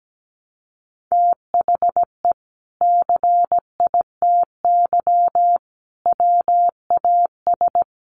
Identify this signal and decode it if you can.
Morse code, heard as a keyed tone, about 17 wpm, 700 Hz. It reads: THE CITY WAS
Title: Sentences from Top 500 Words 17wpm